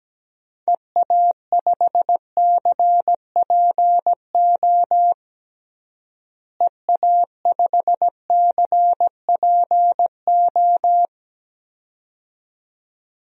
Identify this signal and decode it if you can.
Morse code, heard as a keyed tone, about 17 wpm, 700 Hz. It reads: EA5CPO EA5CPO
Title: DXCC-100 Call Signs - 6 Characters - Repeated 2x - 17wpm